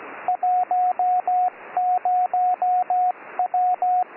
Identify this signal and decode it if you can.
Morse code, heard as a keyed tone, about 17 wpm, 730 Hz, with radio noise. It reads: 10W